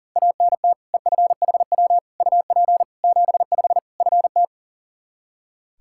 Morse code, heard as a keyed tone, about 40 wpm, 700 Hz. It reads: ANT EFHW UP 75 FT
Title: Practice QSO - Exchange 2 - Info - 40wpm